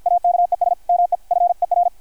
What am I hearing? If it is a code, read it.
Morse code, heard as a keyed tone, about 52 wpm, 700 Hz, with radio noise. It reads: AKES ME FER
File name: archived / CRSnet